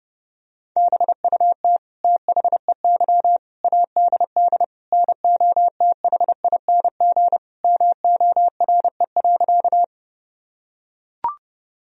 Morse code, heard as a keyed tone, about 30 wpm, 700 Hz. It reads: BUT THEY ADD NOTHING MORE.
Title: Sentences from Top 1000 Words - Review Part 1 - 30wpm